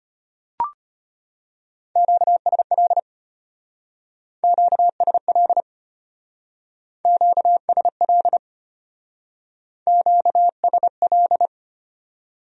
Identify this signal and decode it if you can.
Morse code, heard as a keyed tone, about 32 wpm, 700 Hz. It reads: QSL QSL QSL QSL